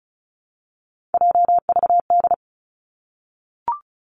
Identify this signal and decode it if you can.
Morse code, heard as a keyed tone, about 36 wpm, 700 Hz. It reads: JVD
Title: Three Challenging Letter-Number Combinations 35wpm